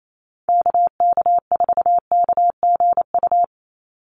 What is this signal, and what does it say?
Morse code, heard as a keyed tone, about 28 wpm, 700 Hz. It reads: KK4KGU
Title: USA Call Signs 28wpm